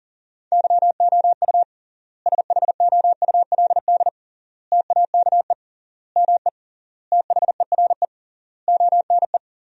Morse code, heard as a keyed tone, about 37 wpm, 700 Hz. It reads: YOU SHOULD TAKE ME THERE ONE
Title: 3x Word-Spacing - Sentences from Top 300 Words - 40wpm